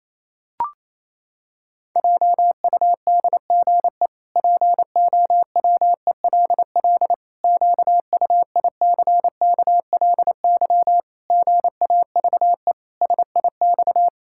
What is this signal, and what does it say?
Morse code, heard as a keyed tone, about 28 wpm, 700 Hz. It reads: JUDGE POWELL QUICKLY GAVE SIX
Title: Pangram Challenge v2 - 28wpm